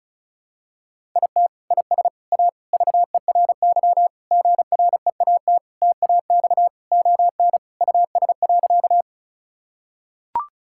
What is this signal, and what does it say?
Morse code, heard as a keyed tone, about 34 wpm, 700 Hz. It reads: IT IS A VERY GREAT TAX ON US.
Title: Sentences from Top 1000 Words - Review Part 5 - 35wpm